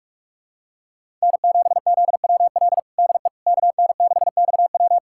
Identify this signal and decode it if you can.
Morse code, heard as a keyed tone, about 41 wpm, 700 Hz. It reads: N7ZWL DE KN6XW
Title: USA Callsign DE Callsign - 5 Characters - 45wpm